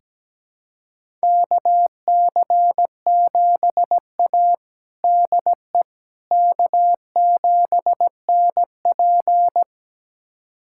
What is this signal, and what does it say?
Morse code, heard as a keyed tone, about 17 wpm, 700 Hz. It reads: KC7A DE K7NP
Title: USA Callsign DE Callsign - 4 Characters - 17wpm